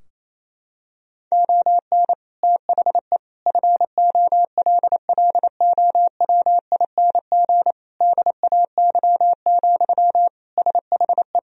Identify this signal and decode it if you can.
Morse code, heard as a keyed tone, about 28 wpm, 700 Hz. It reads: ON THE FOLLOWING DAY, SHE
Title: Sentences from Top 1000 Words - Review Part 4 - 28wpm